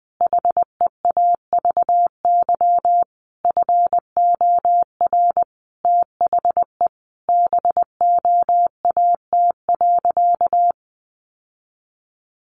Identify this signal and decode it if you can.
Morse code, heard as a keyed tone, about 20 wpm, 700 Hz. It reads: HEAVY FOR THE BOAT.
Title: Sentences from Top 1100 Words 20wpm